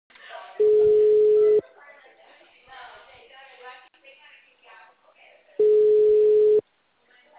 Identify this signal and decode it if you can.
Ringback tone (Europe, 425 Hz), with speech.